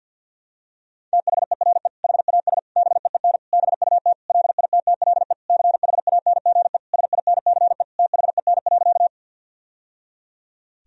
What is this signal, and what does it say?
Morse code, heard as a keyed tone, about 52 wpm, 700 Hz. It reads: THERE HAS BEEN BUT LITTLE CHANGE SINCE THEN.
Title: Sentences from Top 1000 Words - Review Part 1 - 50wpm